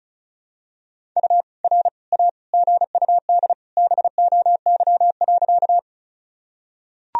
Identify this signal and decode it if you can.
Morse code, heard as a keyed tone, about 36 wpm, 700 Hz. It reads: U R A GUD BOY.
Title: Sentences from Top 200 Words and Abbreviations 35wpm